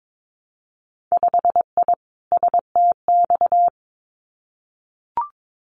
Morse code, heard as a keyed tone, about 22 wpm, 700 Hz. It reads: 5I STX
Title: Winter Field Day Exchanges Encore 22wpm